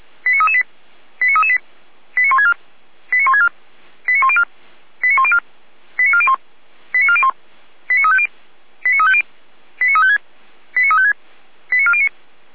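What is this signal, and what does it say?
Five-tone selective call (ZVEI), heard as quick runs of five tones, tones of about 70 ms. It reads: 89338 89338 89164 89164 89104 89104 89401 89401 89350 89350 89356 89356 89409
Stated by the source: Five-tone ZVEI selective call 89701, received by SDR.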